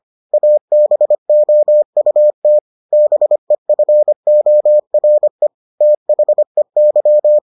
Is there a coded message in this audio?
Yes: Morse code from 0.3 to 7.4 s, heard as a keyed tone, about 25 wpm, 600 Hz. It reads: ABOUT BEFORE THEY